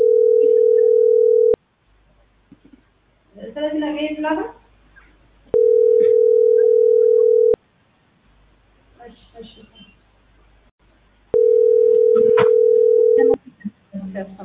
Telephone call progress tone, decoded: ringback tone

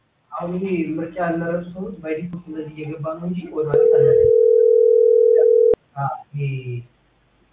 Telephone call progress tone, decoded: ringback tone

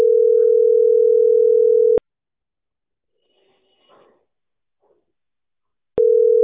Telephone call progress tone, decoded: ringback tone

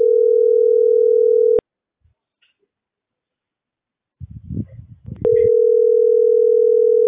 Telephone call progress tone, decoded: ringback tone

